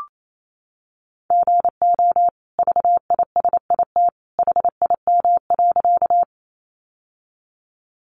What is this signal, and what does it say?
Morse code, heard as a keyed tone, about 28 wpm, 700 Hz. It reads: GO VISIT HIM.
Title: Sentences from Top 900 Words 28wpm